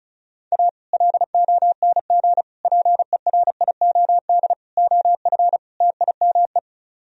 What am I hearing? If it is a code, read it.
Morse code, heard as a keyed tone, about 45 wpm, 700 Hz. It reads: A LONG PERIOD OF TIME